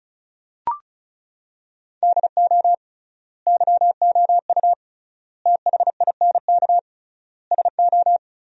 Morse code, heard as a keyed tone, about 36 wpm, 700 Hz. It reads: DO YOU THINK SO
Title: 3x Word-Spacing - Sentences from Top 200 Words - 35wpm